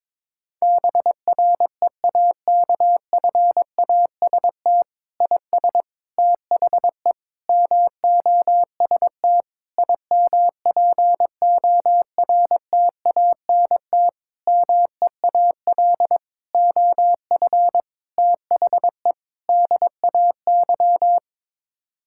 Morse code, heard as a keyed tone, about 22 wpm, 700 Hz. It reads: BREAKFAST IS THE MOST IMPORTANT MEAL OF THE DAY
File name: Sets of 9 Words - 22wpm